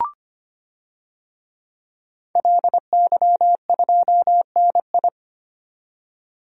Morse code, heard as a keyed tone, about 25 wpm, 700 Hz. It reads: LY2NI
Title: DXCC-100 Call Signs - 5 Characters - 25wpm